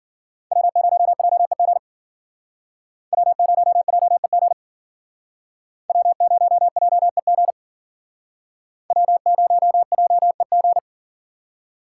Morse code, heard as a keyed tone, about 50 wpm, 700 Hz. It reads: W0JEG W0JEG W0JEG W0JEG